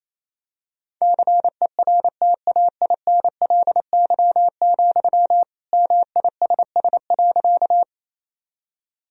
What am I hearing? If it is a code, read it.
Morse code, heard as a keyed tone, about 28 wpm, 700 Hz. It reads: CERTAINLY, MISS.
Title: Sentences from Top 1000 Words - Review Part 3 - 28wpm